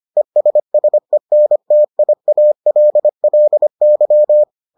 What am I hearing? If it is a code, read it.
Morse code, heard as a keyed tone, about 25 wpm, 600 Hz. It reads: ESSENTIALLY